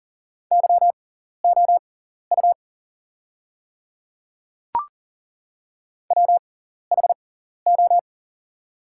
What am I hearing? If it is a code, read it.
Morse code, heard as a keyed tone, about 40 wpm, spaced as for 14 wpm, 690 Hz. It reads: YOU WHO
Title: 6x Character Spacing - Top 100 Words - 40wpm